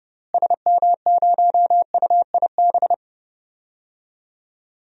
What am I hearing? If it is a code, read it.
Morse code, heard as a keyed tone, about 30 wpm, 700 Hz. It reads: SM0UIB
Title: DXCC-100 Call Signs - 6 Characters - Repeated 3x - 30wpm